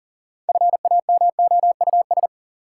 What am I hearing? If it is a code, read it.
Morse code, heard as a keyed tone, about 40 wpm, 700 Hz. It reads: FAMOUS
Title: Six Letter Words - 40wpm